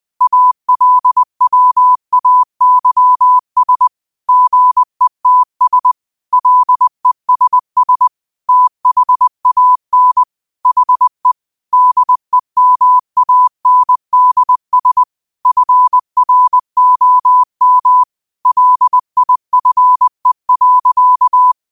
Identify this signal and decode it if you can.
Morse code, heard as a keyed tone, about 20 wpm, 1000 Hz. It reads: ALWAYS GETS LESS THAN HE DEMANDS FROM LIFE.
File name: qotdfiles / 20wpm